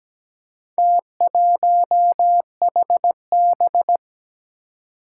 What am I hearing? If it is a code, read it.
Morse code, heard as a keyed tone, about 17 wpm, 700 Hz. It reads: T1HB
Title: Four Challenging Letter-Number Combinations 17wpm